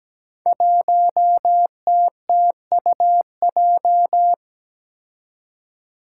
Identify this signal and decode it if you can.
Morse code, heard as a keyed tone, about 17 wpm, 700 Hz. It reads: 1TTUJ
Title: Five Challenging Letter-Number Combinations 17wpm